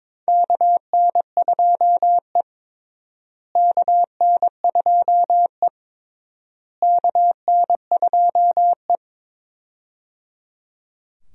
Morse code, heard as a keyed tone, about 21 wpm, 700 Hz. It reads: KN2E KN2E KN2E
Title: USA Call Signs - 4 Characters - Repeated 3x - 22wpm